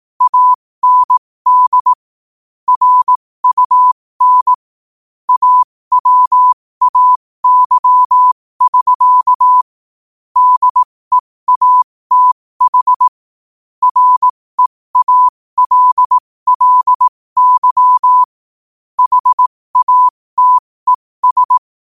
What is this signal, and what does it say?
Morse code, heard as a keyed tone, about 18 wpm, spaced as for 15 wpm, 1000 Hz. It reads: AND RUN AWAY<SK> DEATH REALLY HATES
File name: qotdfiles / 15wpm